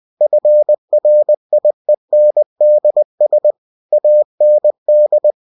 Morse code, heard as a keyed tone, about 20 wpm, 600 Hz. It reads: FRIENDS AND
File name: cw-books / winnie-the-pooh / winnie-20wpm-page-0020